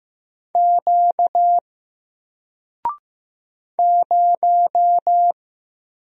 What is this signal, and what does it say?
Morse code, heard as a keyed tone, about 15 wpm, 700 Hz. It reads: Q 0
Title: Single Letter-Number Alt 15wpm